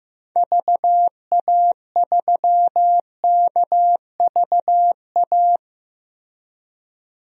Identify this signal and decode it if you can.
Morse code, heard as a keyed tone, about 15 wpm, 700 Hz. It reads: VA3KVA
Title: DXCC-100 Call Signs - 6 Characters - Repeated 2x - 15wpm